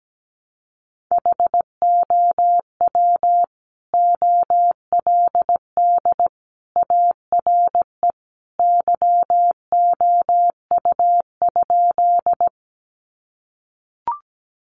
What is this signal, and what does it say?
Morse code, heard as a keyed tone, about 17 wpm, 700 Hz. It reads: HOW OLD ARE YOU?
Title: Sentences from Top 200 Words 17wpm